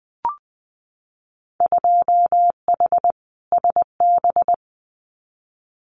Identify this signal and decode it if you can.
Morse code, heard as a keyed tone, about 20 wpm, 700 Hz. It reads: 2H SB
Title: Winter Field Day Exchanges Encore 20wpm